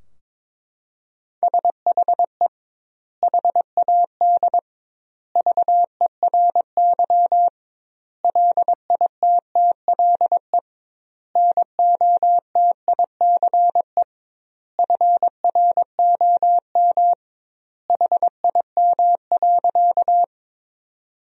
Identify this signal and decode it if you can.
Morse code, heard as a keyed tone, about 23 wpm, 700 Hz. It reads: SHE HAD VERY LITTLE NOTICE FROM HIM.
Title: 2x Word-Spacing - Sentences from Top 400 Words - 22wpm